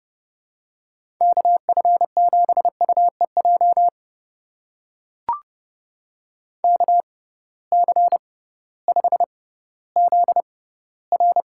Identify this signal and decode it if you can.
Morse code, heard as a keyed tone, about 30 wpm, spaced as for 16 wpm, 700 Hz. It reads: KF7UEJ KC5ZR
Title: USA Call Signs - 6x Character Spacing - 30wpm